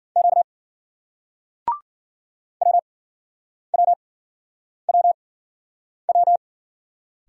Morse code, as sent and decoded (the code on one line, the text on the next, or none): --.- / .-- .-- .-- .--
Q WWWW